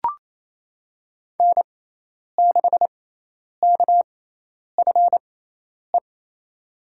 Morse code, heard as a keyed tone, about 27 wpm, spaced as for 10 wpm, 700 Hz. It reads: N6KFE
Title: USA Call Signs - 6x Character Spacing - 28wpm